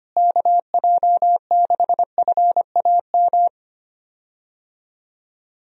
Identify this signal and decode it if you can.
Morse code, heard as a keyed tone, about 25 wpm, 700 Hz. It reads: KJ6FAM
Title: USA Call Signs - 6 Characters - Repeated 2x - 25wpm